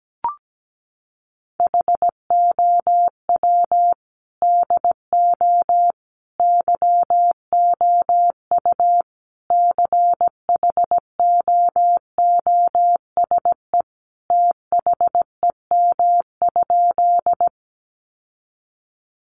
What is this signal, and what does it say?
Morse code, heard as a keyed tone, about 17 wpm, 700 Hz. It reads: HOW DO YOU CHOOSE THEM?
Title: Sentences from Top 700 Words 17wpm